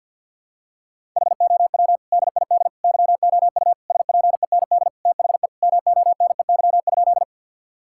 Morse code, heard as a keyed tone, about 49 wpm, 700 Hz. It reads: HOW DID YOU SPEND THE MONEY?
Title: Sentences from Top 1000 Words - Review Part 5 - 50wpm